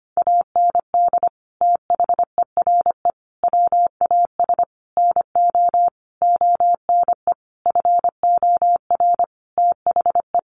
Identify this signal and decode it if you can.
Morse code, heard as a keyed tone, about 25 wpm, 700 Hz. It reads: AND THERE WAS NO ONE FOR THE
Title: Sentences from Top 1500 Words 25wpm